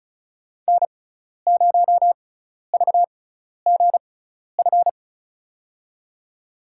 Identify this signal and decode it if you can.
Morse code, heard as a keyed tone, about 34 wpm, spaced as for 12 wpm, 700 Hz. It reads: N0VGF
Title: USA Call Signs - 5 Characters - 6x Character Spacing - 35wpm